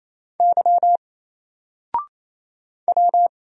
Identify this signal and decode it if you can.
Morse code, heard as a keyed tone, about 28 wpm, 700 Hz. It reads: Y W
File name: Single-Letter-Number-28wpm